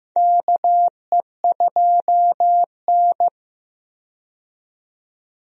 Morse code, heard as a keyed tone, about 15 wpm, 700 Hz. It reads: KE2N
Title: USA Call Signs - 4 Characters - 15wpm